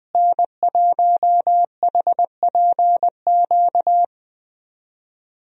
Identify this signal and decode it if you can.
Morse code, heard as a keyed tone, about 21 wpm, 700 Hz. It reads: N1HPQ
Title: USA Call Signs - 5 Characters - Repeated 2x - 20wpm